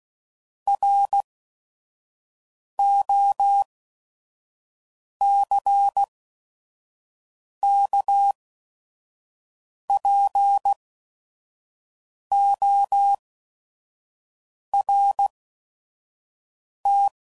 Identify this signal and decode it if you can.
Morse code, heard as a keyed tone, about 16 wpm, spaced as for 5 wpm, 800 Hz. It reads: ROCKPORT
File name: CW Code Groups @5WPM / fromSM's / Cities